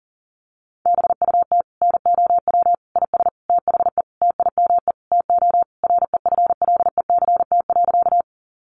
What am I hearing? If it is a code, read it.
Morse code, heard as a keyed tone, about 40 wpm, 700 Hz. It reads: BUT NOW IS THE TIME TO REFLECT.
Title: Sentences from Top 1500 Words 40wpm